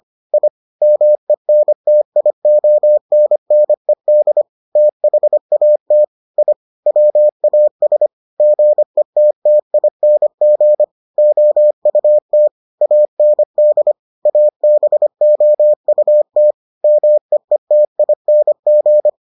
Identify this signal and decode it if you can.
Morse code, heard as a keyed tone, about 25 wpm, 600 Hz. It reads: I MENTIONNED THAT I WAS GETTING OUT AND ABOUT MEETING